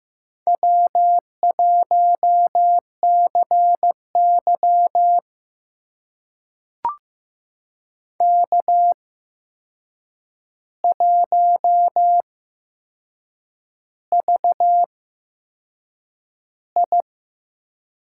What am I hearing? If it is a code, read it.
Morse code, heard as a keyed tone, about 15 wpm, spaced as for 6 wpm, 700 Hz. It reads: W1CY K1VI
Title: USA Call Signs - 4 Characters - 8x Character Spacing - 15wpm